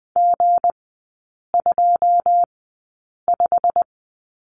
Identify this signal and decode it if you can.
Morse code, heard as a keyed tone, about 20 wpm, spaced as for 8 wpm, 700 Hz. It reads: G25
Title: 184 Lesson 30e - Letter b - Three Characters 20wpm